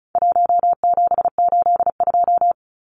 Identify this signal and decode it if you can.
Morse code, heard as a keyed tone, about 34 wpm, 700 Hz. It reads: 1782
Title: Numbers - 4-Digit 35wpm